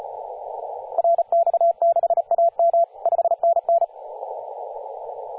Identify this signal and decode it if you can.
Morse code, heard as a keyed tone, about 35 wpm, 680 Hz, with radio noise. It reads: RX6AM 5NN
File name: posts / 2016-sep-22-0537z-kh0-kw2x-18-077mhz-cw